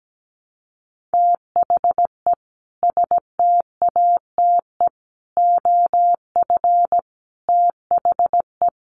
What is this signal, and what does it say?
Morse code, heard as a keyed tone, about 17 wpm, 700 Hz. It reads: THE STATE OF THE
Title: Sets of 5 Words Encore 17wpm